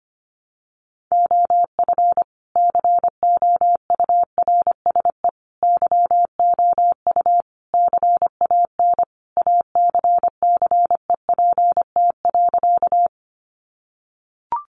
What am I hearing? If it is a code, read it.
Morse code, heard as a keyed tone, about 25 wpm, 700 Hz. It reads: OF COURSE YOU CAN ACCEPT.
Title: Sentences from Top 1300 Words 25wpm